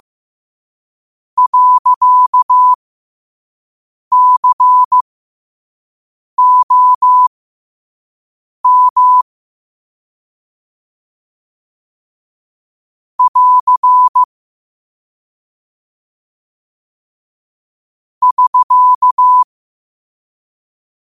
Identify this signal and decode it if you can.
Morse code, heard as a keyed tone, about 15 wpm, spaced as for 5 wpm, 1000 Hz. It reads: .COM <AR> <SK>
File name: 5wpm